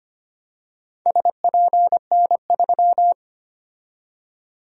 Morse code, heard as a keyed tone, about 25 wpm, 700 Hz. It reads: SPN3
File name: practice / 25wpm / Four Letter-Number Combinations - 25wpm